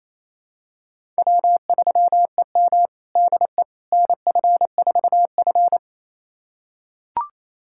Morse code, heard as a keyed tone, about 28 wpm, 700 Hz. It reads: W3EM DE NF4F
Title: USA Callsign DE Callsign - 4 Characters - 28wpm